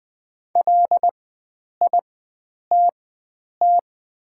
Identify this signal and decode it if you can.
Morse code, heard as a keyed tone, about 20 wpm, spaced as for 9 wpm, 700 Hz. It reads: LITT
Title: 4x Character Spacing - Top 300 Words - 20wpm